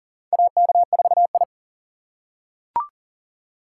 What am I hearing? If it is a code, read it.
Morse code, heard as a keyed tone, about 40 wpm, 700 Hz. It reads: AK4I
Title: USA Call Signs - 4 Characters - 3x Character Spacing - 40wpm